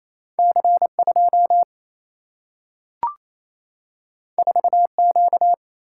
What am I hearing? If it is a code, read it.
Morse code, heard as a keyed tone, about 28 wpm, 700 Hz. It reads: C 2 4 Q
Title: Single-Letter Single-Number 28wpm